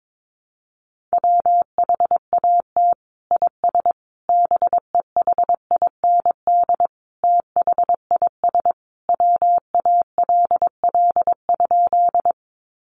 Morse code, heard as a keyed tone, about 22 wpm, 700 Hz. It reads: WHAT IS BEHIND THIS WALL?
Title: Sentences from Top 1100 Words 22wpm